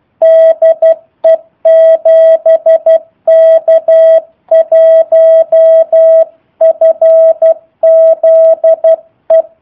Morse code, heard as a keyed tone, about 12 wpm, 650 Hz. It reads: DE 7K1FZE
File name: jh8cbh / 2019cw / sp03- w